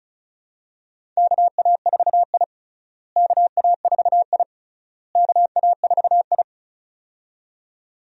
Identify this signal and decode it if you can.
Morse code, heard as a keyed tone, about 36 wpm, 700 Hz. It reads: KA4I KA4I KA4I
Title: USA Call Signs - 4 Characters - Repeated 3x - 35wpm